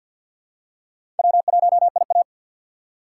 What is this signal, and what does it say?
Morse code, heard as a keyed tone, about 47 wpm, 700 Hz. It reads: W1IA
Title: USA Call Signs - 4 Characters - 2x Character Spacing - 50wpm